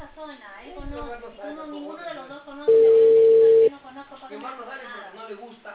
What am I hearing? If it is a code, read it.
Ringback tone (China, 450 Hz), with speech.